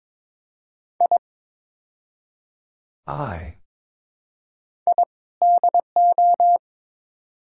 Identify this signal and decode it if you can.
Morse code, heard as a keyed tone, about 21 wpm, 700 Hz. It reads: I I DO
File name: practice / 22wpm / Build A Phrase - 5-Word Phrases - 22wpm